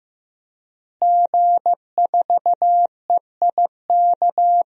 Morse code, heard as a keyed tone, about 15 wpm, 700 Hz. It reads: G4EIK
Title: DXCC-100 Call Signs - 5 Characters - Repeated 2x - 15wpm